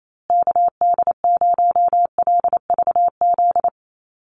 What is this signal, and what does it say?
Morse code, heard as a keyed tone, about 28 wpm, 690 Hz. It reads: KD0LVZ
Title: USA Call Signs Repeated 3x 28wpm